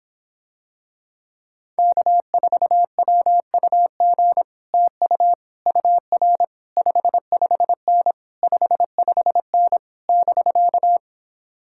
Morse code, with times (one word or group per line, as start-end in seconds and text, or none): 1.8-4.4 K4WUG
4.7-5.3 TU
5.7-6.4 UR
6.8-8.1 55N
8.4-9.8 55N
10.1-11.0 <BK>